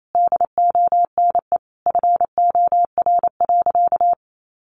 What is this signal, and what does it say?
Morse code, heard as a keyed tone, about 28 wpm, 700 Hz. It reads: DONE FOR.
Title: Sentences from Top 1500 Words 28wpm